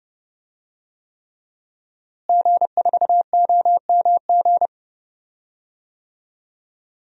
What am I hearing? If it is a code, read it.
Morse code, heard as a keyed tone, about 30 wpm, 700 Hz. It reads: G4OMG
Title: DXCC-100 Call Signs - 5 Characters - Repeated 3x - 30wpm